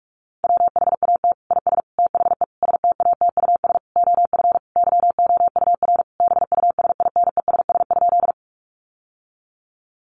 Morse code, heard as a keyed tone, about 44 wpm, 700 Hz. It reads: WHAT IS THE STATUS OF YOUR BUSINESS?
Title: Sentences from Top 1400 Words 45wpm